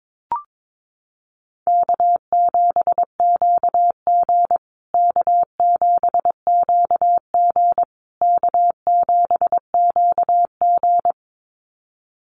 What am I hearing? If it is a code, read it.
Morse code, heard as a keyed tone, about 22 wpm, 690 Hz. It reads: K7QG K7QG K7QG
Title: USA Call Signs Repeated 3x 22wpm